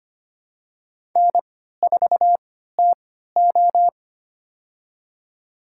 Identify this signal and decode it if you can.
Morse code, heard as a keyed tone, about 25 wpm, spaced as for 14 wpm, 700 Hz. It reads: N4TO
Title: USA Call Signs - 4 Characters - 3x Character Spacing - 25wpm